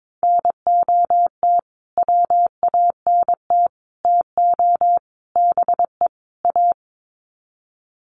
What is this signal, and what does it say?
Morse code, heard as a keyed tone, about 22 wpm, 700 Hz. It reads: NOT WANT TO BE A
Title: Sets of 5 Words Encore 22wpm